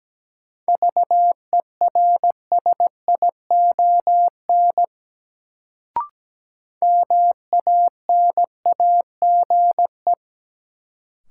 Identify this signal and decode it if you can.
Morse code, heard as a keyed tone, about 17 wpm, 700 Hz. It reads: VERSION MANAGE
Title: Sentences from Top 1400 Words - New Word Drill Alt - 17wpm